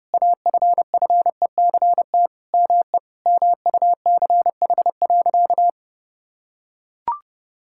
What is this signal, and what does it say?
Morse code, heard as a keyed tone, about 30 wpm, 700 Hz. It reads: AFFECT ME MUCH.
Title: Sentences from Top 1000 Words - Review Part 6 - 30wpm